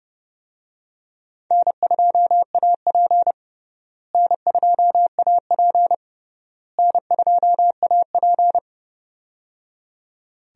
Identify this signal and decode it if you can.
Morse code, heard as a keyed tone, about 30 wpm, 700 Hz. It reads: N2AP N2AP N2AP
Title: USA Call Signs - 4 Characters - Repeated 3x - 30wpm